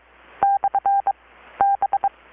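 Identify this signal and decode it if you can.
Morse code, heard as a keyed tone, about 23 wpm, 790 Hz. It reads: / B
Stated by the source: Morse from KPH's Night of Nights XX maritime radiotelegraphy event, received in Chiapas, Mexico about 3,800 km away on a Kenwood R-600 with a grounded 10 m long-wire antenna.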